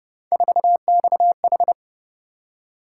Morse code, heard as a keyed tone, about 32 wpm, 700 Hz. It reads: 4XH